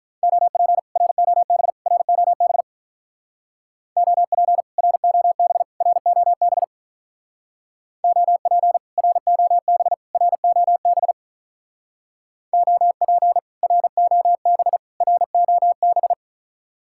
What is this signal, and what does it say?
Morse code, heard as a keyed tone, about 45 wpm, 700 Hz. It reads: OP ROB ROB OP ROB ROB OP ROB ROB OP ROB ROB